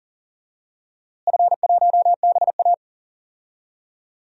Morse code, heard as a keyed tone, about 40 wpm, 700 Hz. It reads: F1BA
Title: DXCC-100 Call Signs - 4 Characters - Repeated 2x - 40wpm